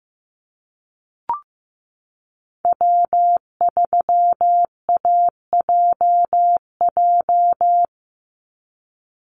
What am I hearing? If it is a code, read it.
Morse code, heard as a keyed tone, about 15 wpm, 700 Hz. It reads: W3AJJ
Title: USA Call Signs 15wpm